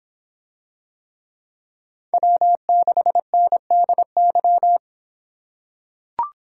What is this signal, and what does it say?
Morse code, heard as a keyed tone, about 26 wpm, 700 Hz. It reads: W6NDY